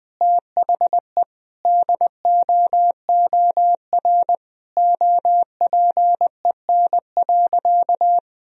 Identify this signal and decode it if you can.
Morse code, heard as a keyed tone, about 20 wpm, 700 Hz. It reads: THE DOOR OPEN.
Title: Sentences from Top 1000 Words - Review Part 1 - 20wpm